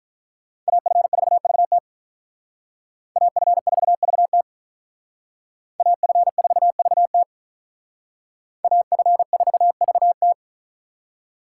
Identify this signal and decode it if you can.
Morse code, heard as a keyed tone, about 45 wpm, 700 Hz. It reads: AF4VT AF4VT AF4VT AF4VT